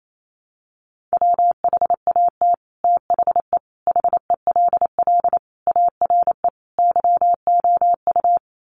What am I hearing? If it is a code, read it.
Morse code, heard as a keyed tone, about 28 wpm, 700 Hz. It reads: WHAT THE HELL ARE YOU
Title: Sets of 5 Words Encore 28wpm